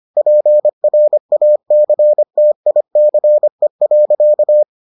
Morse code, heard as a keyed tone, about 25 wpm, 600 Hz. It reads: PRACTICE.